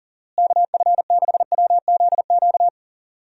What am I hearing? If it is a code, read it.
Morse code, heard as a keyed tone, about 40 wpm, 700 Hz. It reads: KF6WZQ
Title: USA Call Signs - 6 Characters - Repeated 3x - 40wpm